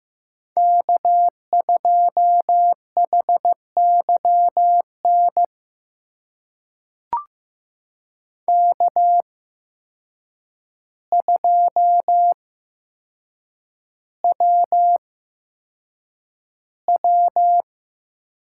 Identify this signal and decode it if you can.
Morse code, heard as a keyed tone, about 15 wpm, spaced as for 7 wpm, 700 Hz. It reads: K2HYN K2WW
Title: USA Call Signs - 8x Character Spacing - 15wpm